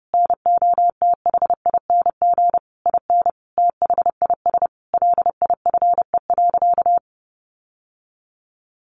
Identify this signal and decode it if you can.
Morse code, heard as a keyed tone, about 30 wpm, 700 Hz. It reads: NOTHING IN THIS LIFE.
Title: Sentences from Top 1200 Words 30wpm